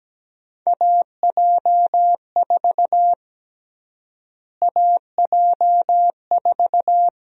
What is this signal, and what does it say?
Morse code, heard as a keyed tone, about 17 wpm, 700 Hz. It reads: AJ4 AJ4
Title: USA Call Sign Prefixes - Repeated 2x - 17wpm